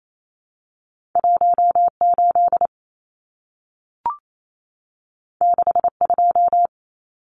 Morse code, heard as a keyed tone, about 28 wpm, 700 Hz. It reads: 18 62
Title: Numbers - 2-Digit 28wpm